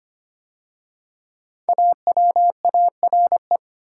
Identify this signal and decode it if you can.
Morse code, heard as a keyed tone, about 25 wpm, 700 Hz. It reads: AWARE
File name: 5 Letter Words - First 3 Letters Given - 25wpm